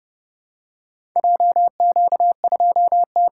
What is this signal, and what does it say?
Morse code, heard as a keyed tone, about 30 wpm, 700 Hz. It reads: JQ2T